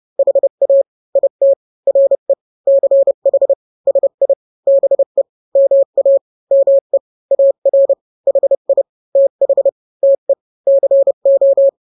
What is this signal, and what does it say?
Morse code, heard as a keyed tone, about 30 wpm, 550 Hz. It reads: HA IT RE CH SI BE MA ME AR HI TH TE CO
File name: n-grams / 2-grams 30 25 WPM